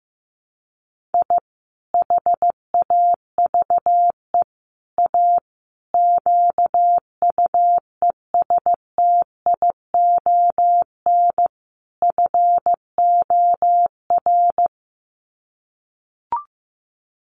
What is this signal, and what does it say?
Morse code, heard as a keyed tone, about 14 wpm, 700 Hz. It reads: I HAVE A QUESTION FOR
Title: Sets of 5 Words Encore2 15wpm